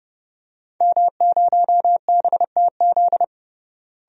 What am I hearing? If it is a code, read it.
Morse code, heard as a keyed tone, about 30 wpm, 700 Hz. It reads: M0BTZ
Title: DXCC-100 Call Signs - 5 Characters - Repeated 3x - 30wpm